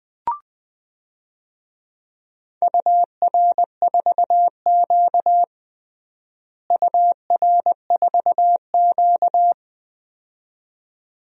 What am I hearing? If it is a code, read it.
Morse code, heard as a keyed tone, about 20 wpm, 700 Hz. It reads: UR4Q UR4Q
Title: DXCC-100 Call Signs - 4 Characters - Repeated 2x - 20wpm